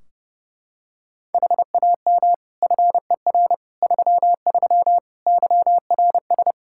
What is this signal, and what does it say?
Morse code, heard as a keyed tone, about 32 wpm, 700 Hz. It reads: HAM FER 33 YRS
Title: Practice QSO - Exchange 3 - More Info - 30wpm